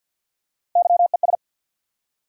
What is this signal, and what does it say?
Morse code, heard as a keyed tone, about 53 wpm, 700 Hz. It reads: YES
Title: Three Letter Words - 50wpm